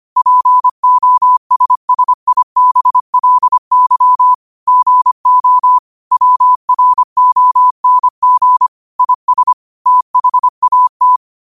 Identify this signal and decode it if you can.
Morse code, heard as a keyed tone, about 25 wpm, 1000 Hz. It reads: POSSIBLY GO WRONG IS THAT
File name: qotdfiles / 25wpm